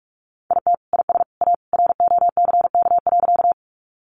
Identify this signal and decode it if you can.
Morse code, heard as a keyed tone, about 44 wpm, 700 Hz. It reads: IT IS A ROCK.
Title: Sentences from Top 400 Words 45wpm